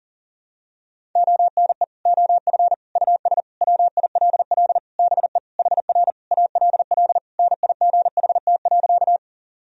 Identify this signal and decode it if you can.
Morse code, heard as a keyed tone, about 40 wpm, 700 Hz. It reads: ONE OF US WILL BE HR ALL NIGHT.
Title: Sentences from Top 300 Words and Abbreviations 40wpm